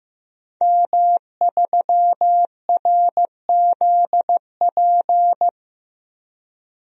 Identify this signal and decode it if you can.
Morse code, heard as a keyed tone, about 15 wpm, 700 Hz. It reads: M3RZP
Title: DXCC-100 Call Signs - 5 Characters - Repeated 2x - 15wpm